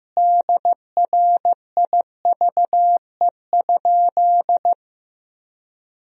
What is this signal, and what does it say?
Morse code, heard as a keyed tone, about 15 wpm, 700 Hz. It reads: DRIVE?
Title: Sentences from Top 1000 Words - Review Part 2 - 15wpm